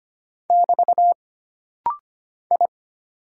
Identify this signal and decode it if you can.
Morse code, heard as a keyed tone, about 25 wpm, 700 Hz. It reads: <BT> I
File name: Single Character (Letter:Number:Punctuation:Prosign) - Rapid-Fire - 25wpm